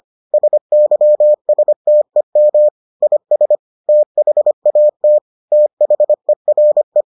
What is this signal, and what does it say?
Morse code, heard as a keyed tone, about 25 wpm, 600 Hz. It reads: SYSTEM IS THAT THERE